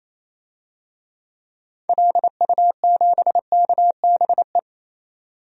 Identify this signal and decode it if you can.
Morse code, heard as a keyed tone, about 28 wpm, 700 Hz. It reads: LU7KBE
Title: DXCC-100 Call Signs - 6 Characters - 28wpm